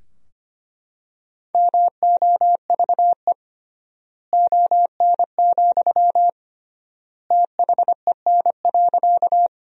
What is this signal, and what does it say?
Morse code, heard as a keyed tone, about 25 wpm, 700 Hz. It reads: MOVE ON, THEN.
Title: 3x Word-Spacing - Sentences from Top 200 Words - 25wpm